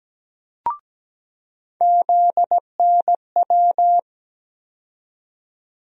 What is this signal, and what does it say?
Morse code, heard as a keyed tone, about 17 wpm, 700 Hz. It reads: ZNW
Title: USA Call Sign Postfixes - 3 Letters 17wpm